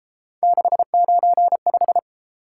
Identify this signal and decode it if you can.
Morse code, heard as a keyed tone, about 32 wpm, 700 Hz. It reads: 695